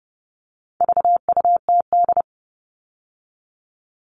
Morse code, heard as a keyed tone, about 30 wpm, 700 Hz. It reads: VUTD
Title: All Cut Numbers - 4-Digit 30wpm